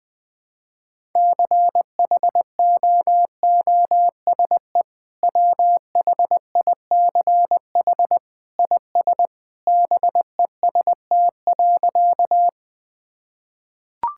Morse code, heard as a keyed tone, about 20 wpm, 700 Hz. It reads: CHOOSE WHICH IS BEST.
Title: Sentences with 2-Word Phrases - 20wpm